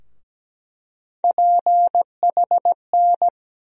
Morse code, heard as a keyed tone, about 17 wpm, 700 Hz. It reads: PHN
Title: USA Call Sign Postfixes - 3 Letters 17wpm